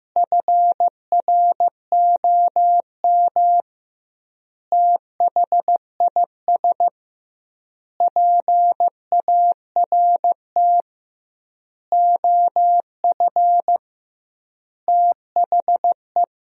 Morse code, heard as a keyed tone, about 15 wpm, 700 Hz. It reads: FROM THIS PART OF THE
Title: 2x Word-Spacing - Sentences from Top 300 Words - 15wpm